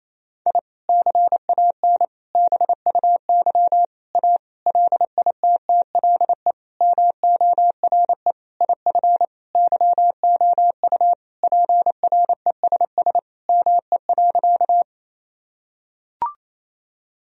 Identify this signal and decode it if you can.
Morse code, heard as a keyed tone, about 28 wpm, 700 Hz. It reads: I CAN BUY A LITTLE MORE IF YOU PRESS ME.
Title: Sentences from Top 1000 Words - Review Part 1 - 28wpm